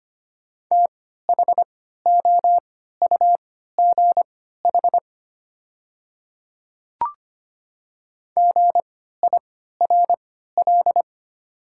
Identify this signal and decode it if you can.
Morse code, heard as a keyed tone, about 25 wpm, spaced as for 14 wpm, 700 Hz. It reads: THOUGH GIRL
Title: 3x Character Spacing - Top 500 Words - 25wpm